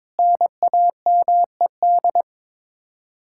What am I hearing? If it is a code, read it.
Morse code, heard as a keyed tone, about 22 wpm, 700 Hz. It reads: NAMED
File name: practice / 22wpm / Sets of 7 Words - 22wpm